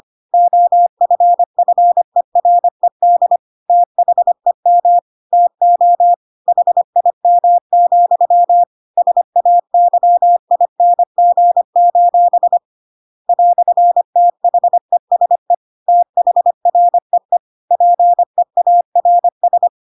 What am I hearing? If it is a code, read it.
Morse code, heard as a keyed tone, about 25 wpm, 700 Hz. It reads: OFFERED THEM TO HIM, SAYING: "THESE THREE PEARS
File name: Pinocchio / Pinocchio ch7 25 25 ik1hge